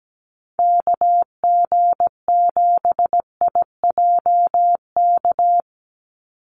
Morse code, heard as a keyed tone, about 17 wpm, 700 Hz. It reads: KG7IJK
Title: USA Call Signs 17wpm